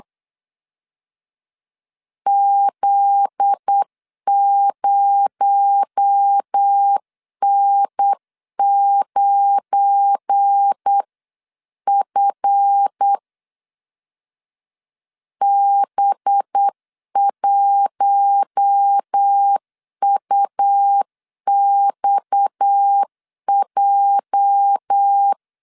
Morse code, heard as a keyed tone, about 8 wpm, 800 Hz. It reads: Z0N9F B1UXJ
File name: LEZIONI / 9 286 8wpm 3425